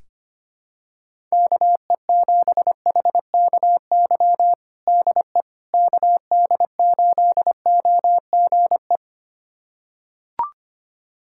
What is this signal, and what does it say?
Morse code, heard as a keyed tone, about 25 wpm, 700 Hz. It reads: KE7HKY DE KD8OGE
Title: USA Callsign DE Callsign - 6 Characters - 25wpm